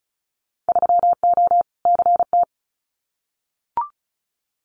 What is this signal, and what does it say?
Morse code, heard as a keyed tone, about 34 wpm, 690 Hz. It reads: 3O CT
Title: Winter Field Day Exchanges 35wpm